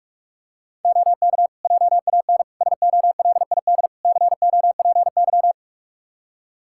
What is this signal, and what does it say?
Morse code, heard as a keyed tone, about 46 wpm, 700 Hz. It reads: OK JAN SOLID COPY
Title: K1USN SST QSO Practice 45wpm